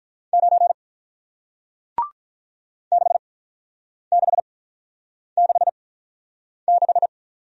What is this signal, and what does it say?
Morse code, heard as a keyed tone, about 48 wpm, 700 Hz. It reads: 9 6 6 6 6